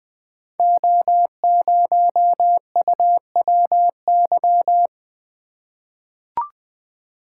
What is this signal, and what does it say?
Morse code, heard as a keyed tone, about 20 wpm, 700 Hz. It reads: O0UWY
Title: Code Groups - 5 Letter:Numbers - 20wpm